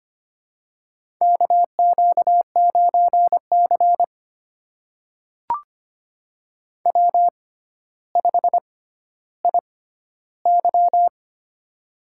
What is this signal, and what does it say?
Morse code, heard as a keyed tone, about 25 wpm, spaced as for 13 wpm, 700 Hz. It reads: KQ9C W5IY
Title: USA Call Signs - 4 Characters - 6x Character Spacing - 25wpm